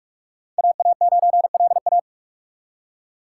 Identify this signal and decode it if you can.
Morse code, heard as a keyed tone, about 59 wpm, spaced as for 52 wpm, 700 Hz. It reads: AA9LA
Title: Speed Racing - USA 2x2 Call Signs 45wpm